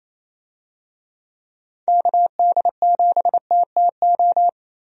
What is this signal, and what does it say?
Morse code, heard as a keyed tone, about 28 wpm, 700 Hz. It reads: KD7TTO
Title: USA Call Signs - 6 Characters - Repeated 3x - 28wpm